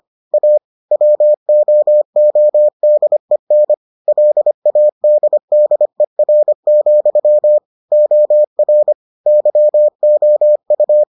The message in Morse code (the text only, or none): A WOODEN LADDER, OR YOU